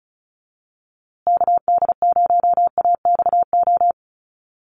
Morse code, heard as a keyed tone, about 34 wpm, 700 Hz. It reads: KD0AXO
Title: USA Call Signs Repeated 2x 35wpm